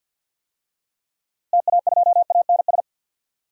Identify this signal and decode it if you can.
Morse code, heard as a keyed tone, about 46 wpm, 700 Hz. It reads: TA2ANS
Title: DXCC-100 Call Signs - 6 Characters - Repeated 3x - 50wpm